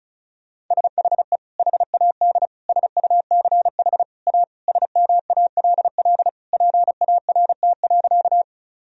Morse code, heard as a keyed tone, about 36 wpm, 700 Hz. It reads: SHE HAD SUCH A SMALL PART.
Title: Sentences from Top 1000 Words - Review Part 1 - 35wpm